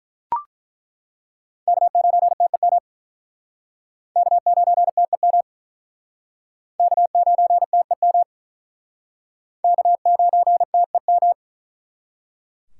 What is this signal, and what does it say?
Morse code, heard as a keyed tone, about 48 wpm, 700 Hz. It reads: K9TEM K9TEM K9TEM K9TEM